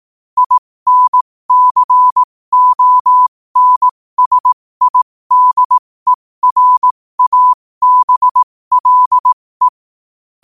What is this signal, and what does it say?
Morse code, heard as a keyed tone, about 18 wpm, spaced as for 15 wpm, 1000 Hz. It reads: INCONSIDERABLE
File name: qotdfiles / 15wpm